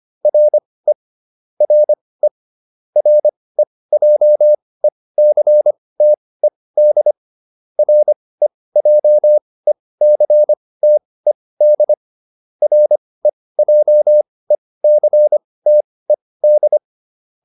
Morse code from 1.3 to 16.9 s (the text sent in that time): RE REJECTED REJECTED REJECTED